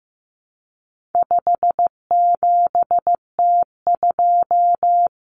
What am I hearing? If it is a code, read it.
Morse code, heard as a keyed tone, about 15 wpm, 700 Hz. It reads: 57T2
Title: Common Cut Numbers - 4-Digit 15wpm